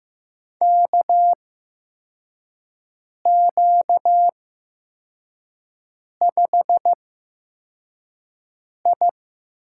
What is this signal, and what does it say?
Morse code, heard as a keyed tone, about 15 wpm, spaced as for 4 wpm, 700 Hz. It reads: KQ5I
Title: USA Call Signs - 4 Characters - 8x Character Spacing - 15wpm